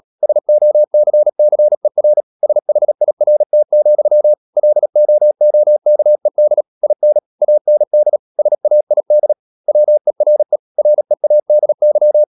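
Morse code, heard as a keyed tone, about 37 wpm, 600 Hz. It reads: SOCCER SHIRT, LOOKED IN AND SAID WERE READY
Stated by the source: Morse code practice at 35–39 wpm sending English text of Ken Follett's novel Never, from CW med Gnister.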